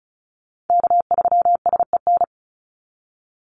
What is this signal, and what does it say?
Morse code, heard as a keyed tone, about 34 wpm, 700 Hz. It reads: K3SEN
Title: Call Signs - 35wpm